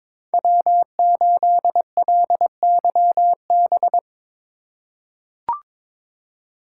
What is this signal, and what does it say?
Morse code, heard as a keyed tone, about 22 wpm, 700 Hz. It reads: W8LYB
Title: USA Call Signs - 5 Characters - 4x Character Spacing - 22wpm